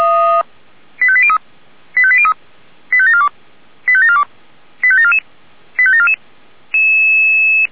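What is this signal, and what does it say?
Five-tone selective call (ZVEI), heard as quick runs of five tones, tones of about 70 ms. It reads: 85803 85803 85742 85742 85744 85744
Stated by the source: ZVEI five-tone call received on an SDR.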